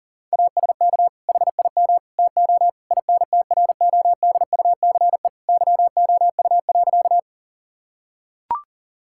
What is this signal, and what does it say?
Morse code, heard as a keyed tone, about 40 wpm, 700 Hz. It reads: ASK HIM TO INTRODUCE YOU.
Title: Sentences from Top 1600 Words 40wpm